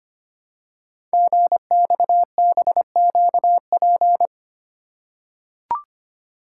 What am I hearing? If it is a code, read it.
Morse code, heard as a keyed tone, about 25 wpm, 700 Hz. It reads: GXBQP
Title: Code Groups - 5 Letters - 25wpm